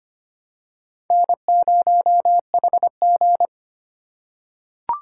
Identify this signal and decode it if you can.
Morse code, heard as a keyed tone, about 25 wpm, 700 Hz. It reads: N0HG
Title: USA Call Signs - 4 Characters - 3x Character Spacing - 25wpm